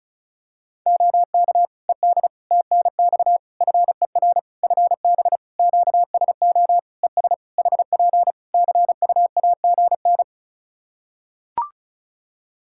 Morse code, heard as a keyed tone, about 34 wpm, 700 Hz. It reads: OK ED TNX FER FB QSO ES HP CUAGN
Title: K1USN SST QSO Practice 35wpm